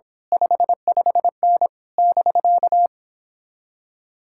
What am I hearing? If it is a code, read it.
Morse code, heard as a keyed tone, about 26 wpm, 700 Hz. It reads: 55N <BK>
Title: Speed-Racing Common QSO Phrases Encore - 22wpm